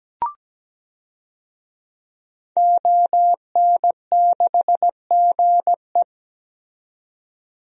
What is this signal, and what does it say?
Morse code, heard as a keyed tone, about 17 wpm, 700 Hz. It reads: ON6GE
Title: DXCC-100 Call Signs - 5 Characters - 17wpm